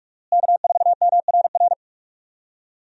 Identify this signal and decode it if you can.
Morse code, heard as a keyed tone, about 41 wpm, 700 Hz. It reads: K4MRR
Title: USA Call Signs - 5 Characters - Repeated 3x - 45wpm